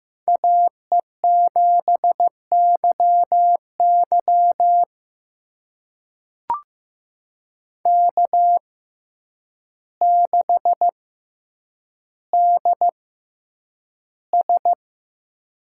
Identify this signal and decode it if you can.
Morse code, heard as a keyed tone, about 15 wpm, spaced as for 8 wpm, 700 Hz. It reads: AE7YY K6DS
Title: USA Call Signs - 6x Character Spacing - 15wpm